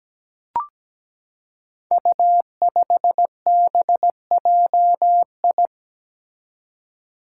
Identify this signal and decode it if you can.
Morse code, heard as a keyed tone, about 17 wpm, 700 Hz. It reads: U5BJI
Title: Five Challenging Letter-Number Combinations 17wpm